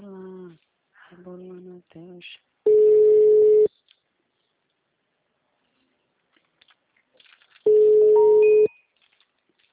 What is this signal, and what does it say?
Ringback tone (Europe, 425 Hz), with speech.
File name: out- -2004- - - .26702